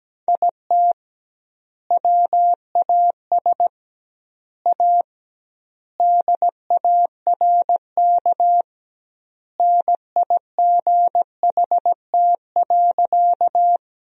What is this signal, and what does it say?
Morse code, heard as a keyed tone, about 17 wpm, 700 Hz. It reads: IT WAS A DARK NIGHT.
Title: 2x Word-Spacing - Sentences from Top 500 Words - 17wpm